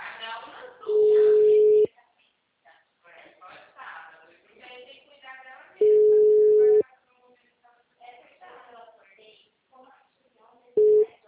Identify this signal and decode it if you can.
Ringback tone (Europe, 425 Hz), with speech.
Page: Arapongas - PR